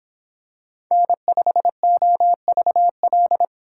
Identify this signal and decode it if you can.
Morse code, heard as a keyed tone, about 26 wpm, 700 Hz. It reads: N5OVL